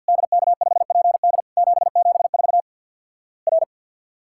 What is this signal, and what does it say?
Morse code, heard as a keyed tone, about 50 wpm, 700 Hz. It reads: DK5PD 774 R
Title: Worked All Europe DX Contest - QTC Exchange with Repeats - 50wpm